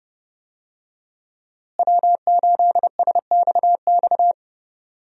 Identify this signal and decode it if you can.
Morse code, heard as a keyed tone, about 30 wpm, 700 Hz. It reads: W8SXX
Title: USA Call Signs - 5 Characters - Repeated 3x - 30wpm